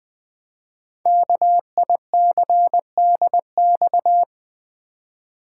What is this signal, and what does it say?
Morse code, heard as a keyed tone, about 21 wpm, 700 Hz. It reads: KICDX
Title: Code Groups - 5 Letter:Numbers - 20wpm